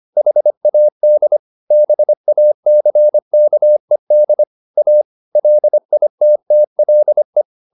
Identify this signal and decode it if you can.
Morse code, heard as a keyed tone, about 25 wpm, 600 Hz. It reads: HAD BACKED A LITTLE